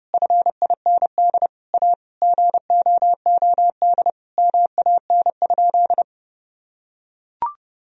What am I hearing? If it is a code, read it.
Morse code, heard as a keyed tone, about 32 wpm, 700 Hz. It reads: FIND A GOOD MAN?
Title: Sentences from Top 1000 Words - Review Part 1 - 30wpm